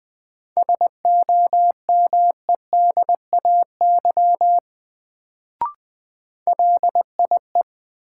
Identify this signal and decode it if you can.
Morse code, heard as a keyed tone, about 20 wpm, 700 Hz. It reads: SOMEDAY LIE
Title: Sentences from Top 1600 Words - New Word Drill - 20wpm